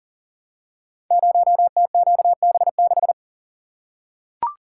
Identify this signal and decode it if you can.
Morse code, heard as a keyed tone, about 41 wpm, 700 Hz. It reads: 0TQB6
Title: Code Groups - 5 Letter:Numbers - 40wpm